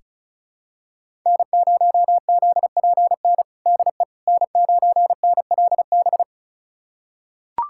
Morse code, heard as a keyed tone, about 36 wpm, 700 Hz. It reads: N0ZPN DE N9NLB
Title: USA Callsign DE Callsign 35wpm